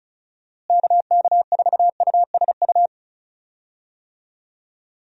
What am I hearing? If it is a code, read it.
Morse code, heard as a keyed tone, about 34 wpm, 700 Hz. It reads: KK4USU
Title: USA Call Signs - 6 Characters - Repeated 3x - 35wpm